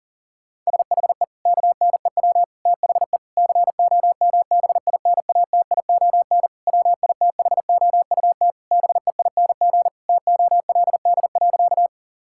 Morse code, heard as a keyed tone, about 40 wpm, 700 Hz. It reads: SHE KNEW THE COMBINATION WITHOUT BEING TOLD.
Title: Sentences from Top 1000 Words - Review Part 6 - 40wpm